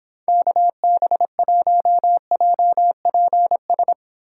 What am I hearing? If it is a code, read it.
Morse code, heard as a keyed tone, about 26 wpm, 700 Hz. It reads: KB1JPS